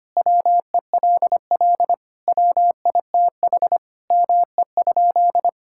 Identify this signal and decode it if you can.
Morse code, heard as a keyed tone, about 25 wpm, 700 Hz. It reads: WELL WITH ME?
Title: Sentences from Top 1600 Words 25wpm